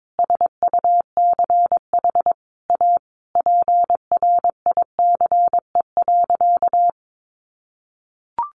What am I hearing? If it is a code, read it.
Morse code, heard as a keyed tone, about 22 wpm, 700 Hz. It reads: SUCH A PRICE.
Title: Sentences from Top 1100 Words 22wpm